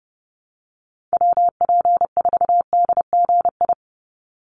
Morse code, heard as a keyed tone, about 30 wpm, 700 Hz. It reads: WP4DGI
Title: USA Call Signs 30wpm